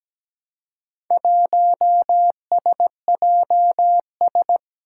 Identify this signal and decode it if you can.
Morse code, heard as a keyed tone, about 17 wpm, 700 Hz. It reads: 1SJS
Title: Four Challenging Letter-Number Combinations 17wpm